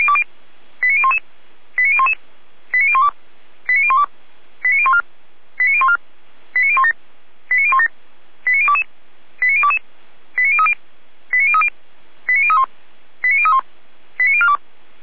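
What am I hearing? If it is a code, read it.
Five-tone selective call (ZVEI), heard as quick runs of five tones, tones of about 70 ms. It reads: 89011 89011 89012 89012 89014 89014 89017 89017 89022 89022 89030 89030 89031 89031 89042
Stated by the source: Five-tone ZVEI selective call received with an SDR, recorded by Feuerwehr Orlamünde.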